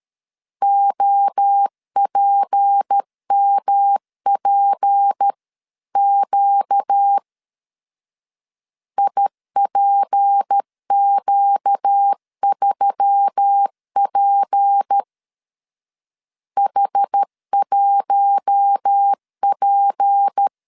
Morse code, heard as a keyed tone, about 13 wpm, 800 Hz. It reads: OPMPQ IPQ3P H1P
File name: LEZIONI / 6 PKX0 / 6 P 12wpm 2033